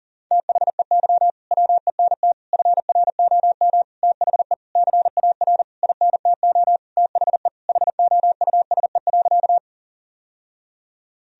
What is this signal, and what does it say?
Morse code, heard as a keyed tone, about 40 wpm, 700 Hz. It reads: THEY WENT FROM THE CAR INTO THE HOUSE.
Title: Sentences from Top 1000 Words - Review Part 1 - 40wpm